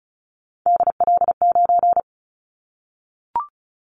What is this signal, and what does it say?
Morse code, heard as a keyed tone, about 36 wpm, 700 Hz. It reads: DL9
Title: Callsign Prefixes for Top 25 DXCC - 35wpm